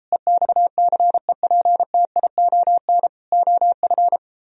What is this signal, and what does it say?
Morse code, heard as a keyed tone, about 35 wpm, 700 Hz. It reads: EXCEPTION OF
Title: Speed-Racing - Sets of 4 Words 28wpm